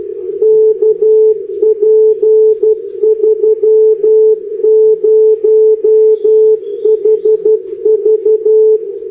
Morse code, heard as a keyed tone, about 11 wpm, 430 Hz, with radio noise. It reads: KP30HV
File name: beacons / OH5SHF